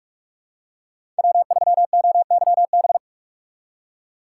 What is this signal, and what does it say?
Morse code, heard as a keyed tone, about 46 wpm, 700 Hz. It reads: W 3 O Y B
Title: USA Call Signs - 5 Characters - 6x Character Spacing - 45wpm